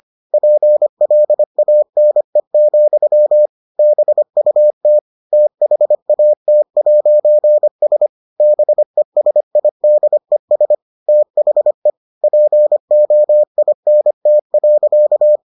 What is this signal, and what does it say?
Morse code, heard as a keyed tone, about 25 wpm, 600 Hz. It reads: PLANE, BUT THAT'S BESIDES THE POINT.